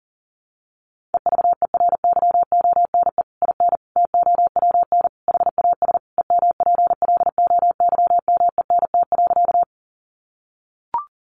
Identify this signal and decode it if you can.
Morse code, heard as a keyed tone, about 40 wpm, 700 Hz. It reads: EVERYONE IN TOWN HAS EMPLOYMENT.
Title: Sentences from Top 1500 Words 40wpm